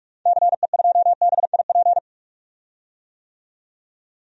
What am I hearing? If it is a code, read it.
Morse code, heard as a keyed tone, about 46 wpm, 700 Hz. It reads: CE2BIP
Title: DXCC-100 Call Signs - 6 Characters - 45wpm